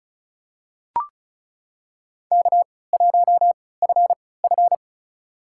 Morse code, heard as a keyed tone, about 36 wpm, spaced as for 20 wpm, 700 Hz. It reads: K1FF
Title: USA Call Signs - 3x Character Spacing - 35wpm